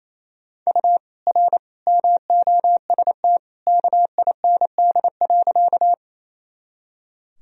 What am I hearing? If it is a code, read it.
Morse code, heard as a keyed tone, about 28 wpm, 700 Hz. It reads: U R MOST KIND.
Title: Sentences from Top 200 Words and Abbreviations 28wpm